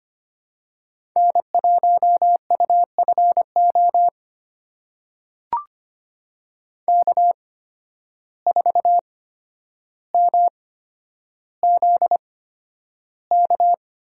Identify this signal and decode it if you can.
Morse code, heard as a keyed tone, about 25 wpm, spaced as for 11 wpm, 700 Hz. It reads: N1UFO K4MZK
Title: USA Call Signs - 5 Characters - 8x Character Spacing - 25wpm